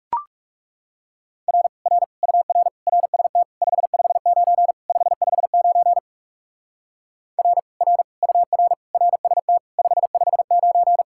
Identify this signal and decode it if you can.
Morse code, heard as a keyed tone, about 38 wpm, 700 Hz. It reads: R R UR RST 559 559 R R UR RST 559